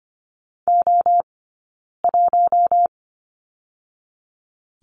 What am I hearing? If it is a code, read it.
Morse code, heard as a keyed tone, about 25 wpm, 700 Hz. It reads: O 1
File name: Lesson 15d - Number 5 - Two Characters 25wpm